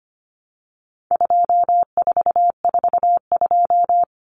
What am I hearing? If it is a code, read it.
Morse code, heard as a keyed tone, about 25 wpm, 700 Hz. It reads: 2442
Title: Numbers - 4-Digit 25wpm